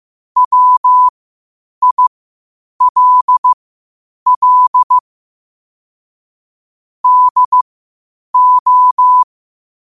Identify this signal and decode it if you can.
Morse code, heard as a keyed tone, about 15 wpm, spaced as for 8 wpm, 1000 Hz. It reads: WILL DO
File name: qotdfiles / 8wpm